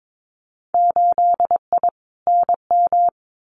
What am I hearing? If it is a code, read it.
Morse code, heard as a keyed tone, about 22 wpm, 700 Hz. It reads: 8I NM
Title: Winter Field Day Exchanges Encore 22wpm